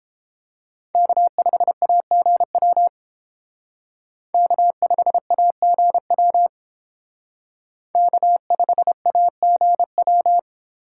Morse code, heard as a keyed tone, about 30 wpm, 700 Hz. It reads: K5AGW K5AGW K5AGW